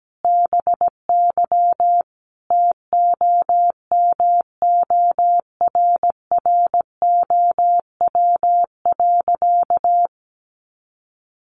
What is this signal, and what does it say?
Morse code, heard as a keyed tone, about 17 wpm, 700 Hz. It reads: BY TOMORROW.
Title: Sentences from Top 1100 Words 17wpm